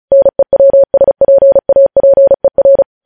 Morse code, heard as a keyed tone, about 36 wpm, 550 Hz. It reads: NEWSPAPER
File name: podcast / News headlines-1x-35x35